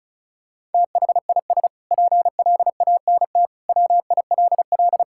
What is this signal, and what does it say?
Morse code, heard as a keyed tone, about 36 wpm, 700 Hz. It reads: THIS PLANT WILL
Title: Sentences from Top 1000 Words - Review Part 6 - 35wpm